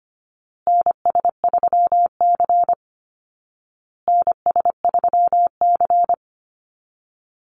Morse code, heard as a keyed tone, about 25 wpm, 700 Hz. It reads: NS3C NS3C
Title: Call Signs Repeated 2x - 25wpm